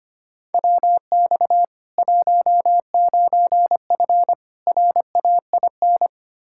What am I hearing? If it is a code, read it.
Morse code, heard as a keyed tone, about 25 wpm, 700 Hz. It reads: WX 19F RAIN
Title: K1USN SST QSO Practice 25wpm